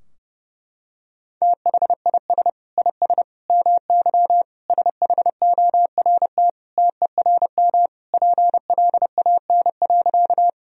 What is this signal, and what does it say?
Morse code, heard as a keyed tone, about 30 wpm, 700 Hz. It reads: THIS IS MY SHORT TERM PLAN.
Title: Sentences from Top 1000 Words - Review Part 4 - 30wpm